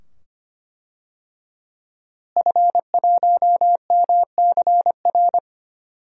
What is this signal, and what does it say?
Morse code, heard as a keyed tone, about 25 wpm, 700 Hz. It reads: F1MCR
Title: DXCC-100 Call Signs - 5 Characters - Repeated 2x - 25wpm